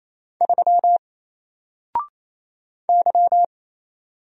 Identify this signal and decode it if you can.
Morse code, heard as a keyed tone, about 27 wpm, 700 Hz. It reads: 3 Y
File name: Single-Letter-Number-28wpm